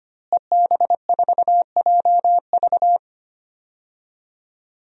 Morse code, heard as a keyed tone, about 25 wpm, 700 Hz. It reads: EB4JV
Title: DXCC-100 Call Signs - 5 Characters - Repeated 3x - 25wpm